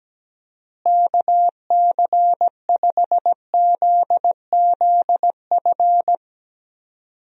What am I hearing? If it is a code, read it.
Morse code, heard as a keyed tone, about 17 wpm, 700 Hz. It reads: KC5ZZF
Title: USA Call Signs - 6 Characters - Repeated 3x - 17wpm